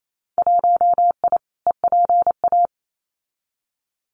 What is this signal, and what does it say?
Morse code, heard as a keyed tone, about 28 wpm, 700 Hz. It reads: 1I EPA
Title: Winter Field Day Exchanges Encore 28wpm